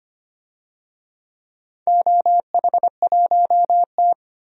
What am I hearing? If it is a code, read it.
Morse code, heard as a keyed tone, about 25 wpm, 700 Hz. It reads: OH1T
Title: DXCC-100 Call Signs - 4 Characters - 25wpm